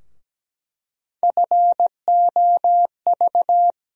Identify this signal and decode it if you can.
Morse code, heard as a keyed tone, about 17 wpm, 700 Hz. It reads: FOV
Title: USA Call Sign Postfixes - 3 Letters 17wpm